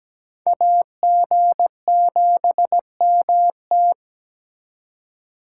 Morse code, heard as a keyed tone, about 20 wpm, 700 Hz. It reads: AG7MT
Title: Speed Racing - USA 2x2 Call Signs 17wpm